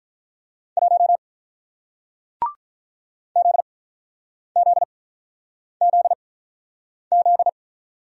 Morse code, heard as a keyed tone, about 48 wpm, spaced as for 9 wpm, 700 Hz. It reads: 1 ZZZZ